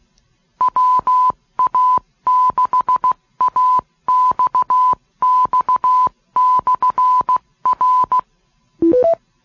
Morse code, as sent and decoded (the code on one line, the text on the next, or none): .-- .- -.... .- -..- -..- -..-. .-.
WA6AXX/R